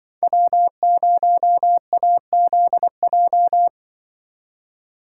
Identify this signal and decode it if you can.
Morse code, heard as a keyed tone, about 24 wpm, 700 Hz. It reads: W0AZJ
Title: Speed Racing - USA 1x3 Call Signs 20wpm